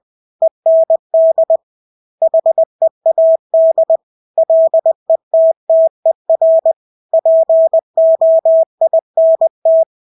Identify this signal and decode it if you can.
Morse code, heard as a keyed tone, about 20 wpm, 650 Hz. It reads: END HEAD LETTER POINT
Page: Engelske ord 20 wpm